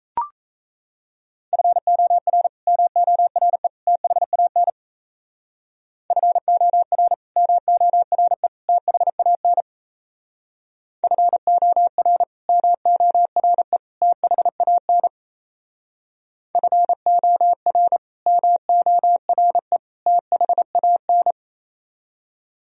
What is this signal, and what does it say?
Morse code, heard as a keyed tone, about 37 wpm, 700 Hz. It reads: FOR MORE THAN FOR MORE THAN FOR MORE THAN FOR MORE THAN